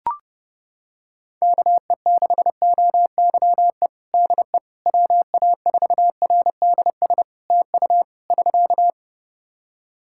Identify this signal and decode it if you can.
Morse code, heard as a keyed tone, about 30 wpm, 700 Hz. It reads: KE6OYE DE WA4RDS TU <SK>
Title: Practice QSO - Exchange 4 - Ending - 30wpm